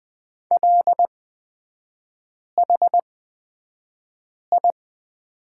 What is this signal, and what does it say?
Morse code, heard as a keyed tone, about 20 wpm, spaced as for 5 wpm, 690 Hz. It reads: LHI
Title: 229 Lesson 37e - Letter x - Three Characters 20wpm